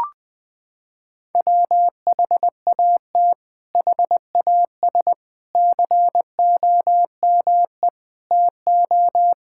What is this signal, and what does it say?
Morse code, heard as a keyed tone, about 20 wpm, 700 Hz. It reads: WHAT HAS COME TO
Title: Sets of 8 Words - 20wpm